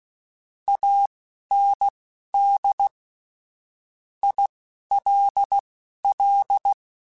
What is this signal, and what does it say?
Morse code, heard as a keyed tone, about 16 wpm, spaced as for 11 wpm, 800 Hz. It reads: AND ILL
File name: QSOs / QSOs-mp3s / kc7haa@10